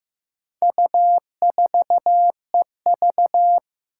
Morse code, heard as a keyed tone, about 15 wpm, 700 Hz. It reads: U4EV
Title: Four Challenging Letter-Number Combinations 15wpm